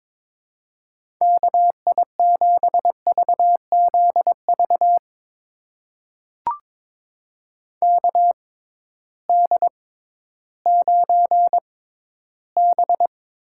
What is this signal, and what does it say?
Morse code, heard as a keyed tone, about 22 wpm, spaced as for 13 wpm, 700 Hz. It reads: KI7VZV KD9B
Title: USA Call Signs - 6 Characters - 6x Character Spacing - 22wpm